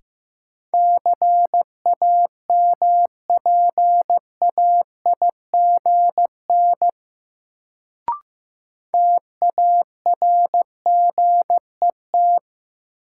Morse code, heard as a keyed tone, about 15 wpm, 700 Hz. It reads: CAMPAIGN TARGET
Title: Sentences from Top 1200 Words - New Word Drill Alt - 15wpm